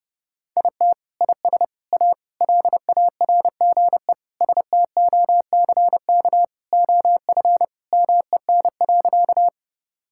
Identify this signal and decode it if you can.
Morse code, heard as a keyed tone, about 30 wpm, 700 Hz. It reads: IT IS A LARGE STOCK OF MEN.
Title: Sentences from Top 1000 Words - Review Part 5 - 30wpm